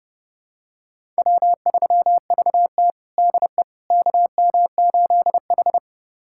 Morse code, heard as a keyed tone, about 30 wpm, 700 Hz. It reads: W3VT DE KM8H
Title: USA Callsign DE Callsign - 4 Characters - 30wpm